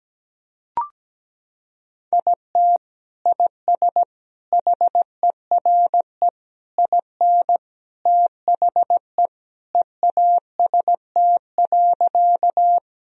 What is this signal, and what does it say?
Morse code, heard as a keyed tone, about 17 wpm, 700 Hz. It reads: IT IS HERE IN THE EAST.
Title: Sentences from Top 1000 Words - Review Part 3 - 17wpm